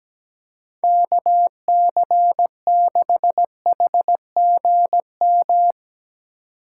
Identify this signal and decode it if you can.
Morse code, heard as a keyed tone, about 17 wpm, 700 Hz. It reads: KC6HGM
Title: Speed Racing - USA 2x3 Call Signs 17wpm